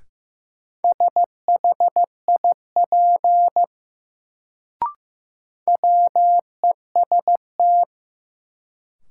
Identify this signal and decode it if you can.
Morse code, heard as a keyed tone, about 15 wpm, 700 Hz. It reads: SHIP WEST
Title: Sentences from Top 400 Words - New Word Drill - 15wpm